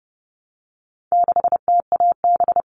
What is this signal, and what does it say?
Morse code, heard as a keyed tone, about 30 wpm, 700 Hz. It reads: 6TAB
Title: All Cut Numbers - 4-Digit 30wpm